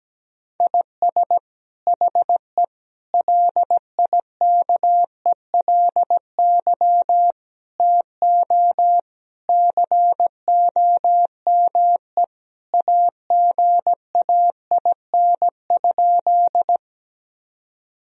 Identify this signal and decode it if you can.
Morse code, heard as a keyed tone, about 17 wpm, 700 Hz. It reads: IS HE LIKELY TO COME AGAIN?
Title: Sentences from Top 1000 Words - Review Part 4 - 17wpm